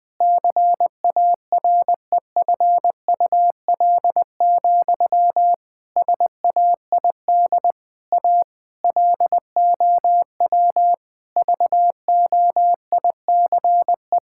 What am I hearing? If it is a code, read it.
Morse code, heard as a keyed tone, about 20 wpm, 700 Hz. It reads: CAREFUL, SAID A LOW VOICE
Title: Sentences from Top 1600 Words 20wpm